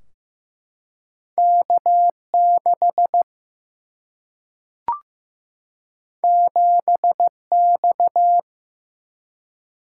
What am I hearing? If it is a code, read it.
Morse code, heard as a keyed tone, about 15 wpm, 700 Hz. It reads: K6 7X
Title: Single-Letter Single-Number 15wpm